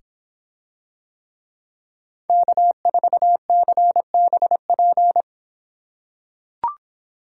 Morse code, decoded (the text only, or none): K4CBP